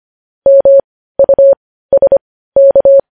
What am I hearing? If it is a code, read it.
Morse code, heard as a keyed tone, about 25 wpm, spaced as for 15 wpm, 550 Hz. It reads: MUSK
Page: News Headlines in Morse Code at 15 WPM